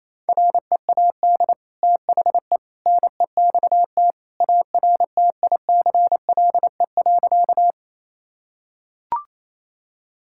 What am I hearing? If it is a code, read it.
Morse code, heard as a keyed tone, about 28 wpm, 700 Hz. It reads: READ THE NEXT ARTICLE.
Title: Sentences from Top 1000 Words - Review Part 5 - 28wpm